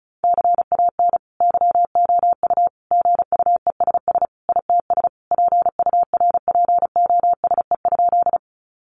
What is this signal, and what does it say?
Morse code, heard as a keyed tone, about 34 wpm, 700 Hz. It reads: CAN YOU GUESS ITS PURPOSE?
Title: Sentences from Top 1500 Words 35wpm